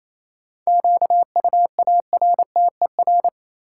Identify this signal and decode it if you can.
Morse code, heard as a keyed tone, about 28 wpm, 700 Hz. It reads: QUARTER
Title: Sentences from Top 1300 Words - New Word Drill Alt - 28wpm